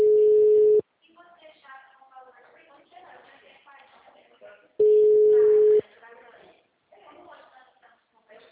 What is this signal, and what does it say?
Ringback tone (Europe, 425 Hz), with speech.